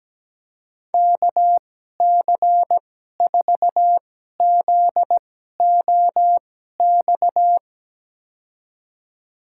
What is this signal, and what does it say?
Morse code, heard as a keyed tone, about 17 wpm, spaced as for 12 wpm, 700 Hz. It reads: KC4ZOX
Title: USA Call Signs - 6 Characters - 2x Character Spacing - 17wpm